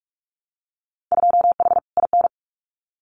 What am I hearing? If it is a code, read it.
Morse code, heard as a keyed tone, about 46 wpm, 700 Hz. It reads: 2H IN
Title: Winter Field Day Exchanges Encore 45wpm